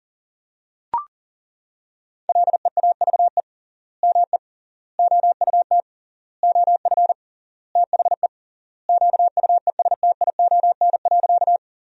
Morse code, heard as a keyed tone, about 40 wpm, 700 Hz. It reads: LEAVE ME OUT OF THE QUESTION.
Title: 3x Word-Spacing - Sentences from Top 400 Words - 40wpm